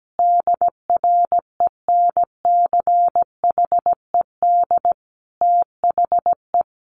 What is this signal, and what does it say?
Morse code, heard as a keyed tone, about 17 wpm, 700 Hz. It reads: DRENCHED THE
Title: A Sound Of Thunder 17wpm